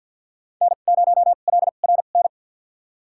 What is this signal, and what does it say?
Morse code, heard as a keyed tone, about 52 wpm, spaced as for 36 wpm, 700 Hz. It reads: N0LRN
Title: USA Call Signs - 5 Characters - 2x Character Spacing - 50wpm